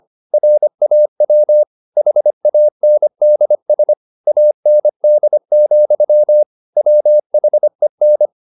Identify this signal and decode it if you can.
Morse code, heard as a keyed tone, about 25 wpm, 600 Hz. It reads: RAW HANDS AND, WHEN